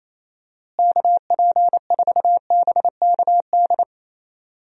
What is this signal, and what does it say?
Morse code, heard as a keyed tone, about 28 wpm, 700 Hz. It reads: KP4BKD
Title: DXCC-100 Call Signs - 6 Characters - Repeated 3x - 28wpm